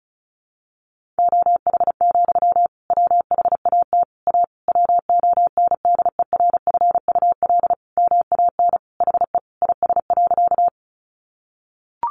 Morse code, heard as a keyed tone, about 34 wpm, 700 Hz. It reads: OH, WHAT A WONDERFUL MAN HE IS.
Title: Sentences from Top 1100 Words 35wpm